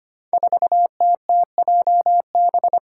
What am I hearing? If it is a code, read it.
Morse code, heard as a keyed tone, about 25 wpm, 700 Hz. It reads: 4TTJB
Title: Five Challenging Letter-Number Combinations 25wpm